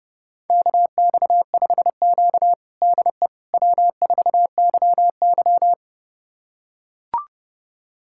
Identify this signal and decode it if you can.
Morse code, heard as a keyed tone, about 30 wpm, 700 Hz. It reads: KX5Q DE W4YY
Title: USA Callsign DE Callsign - 4 Characters - 30wpm